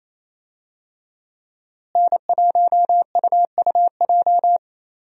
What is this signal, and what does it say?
Morse code, heard as a keyed tone, about 28 wpm, 700 Hz. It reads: N1UUJ
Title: USA Call Signs - 5 Characters - Repeated 3x - 28wpm